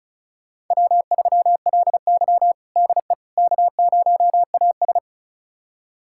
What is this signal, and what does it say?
Morse code, heard as a keyed tone, about 36 wpm, 700 Hz. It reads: W3LY DE K0AS
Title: USA Callsign DE Callsign - 4 Characters - 35wpm